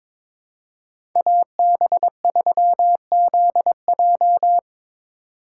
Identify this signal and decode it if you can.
Morse code, heard as a keyed tone, about 22 wpm, 700 Hz. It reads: AB3ZJ
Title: USA Call Signs - 5 Characters - 4x Character Spacing - 22wpm